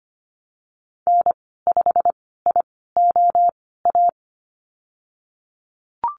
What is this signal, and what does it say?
Morse code, heard as a keyed tone, about 25 wpm, spaced as for 16 wpm, 700 Hz. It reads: N5IOA
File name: Lesson 15g - Number 5 - Call Signs 25wpm